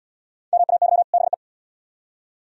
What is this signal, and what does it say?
Morse code, heard as a keyed tone, about 69 wpm, 700 Hz. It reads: MAY BE